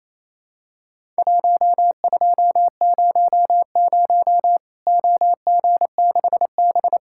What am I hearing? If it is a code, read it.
Morse code, heard as a keyed tone, about 27 wpm, 700 Hz. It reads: 1200 OG6B
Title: Worked All Europe DX Contest - QTC Exchange - 28wpm